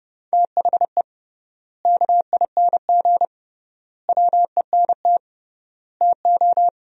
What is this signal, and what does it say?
Morse code, heard as a keyed tone, about 32 wpm, 700 Hz. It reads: THE KING WENT TO
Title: 3x Word-Spacing - Sentences from Top 400 Words - 30wpm